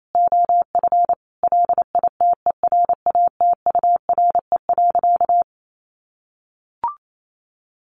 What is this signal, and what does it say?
Morse code, heard as a keyed tone, about 28 wpm, 700 Hz. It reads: OF LITERATURE.
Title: Sentences from Top 900 Words 28wpm